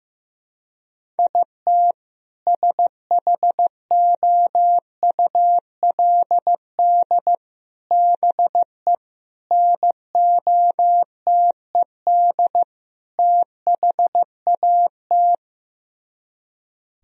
Morse code, heard as a keyed tone, about 15 wpm, 700 Hz. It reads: IT SHOULD BE NOTED THAT
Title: Speed-Racing - Sets of 5 Words 15wpm